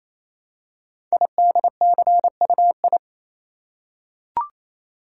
Morse code, heard as a keyed tone, about 27 wpm, 700 Hz. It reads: IDCUI
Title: Code Groups - 5 Letter:Numbers - 28wpm